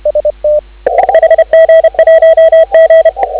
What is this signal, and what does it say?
Morse code, heard as a keyed tone, about 30 wpm, 3250 Hz. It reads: HG1G
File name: CW / 2017 HA-OB HA5YG